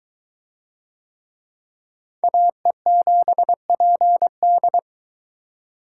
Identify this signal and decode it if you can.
Morse code, heard as a keyed tone, about 21 wpm, 700 Hz. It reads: AE7PD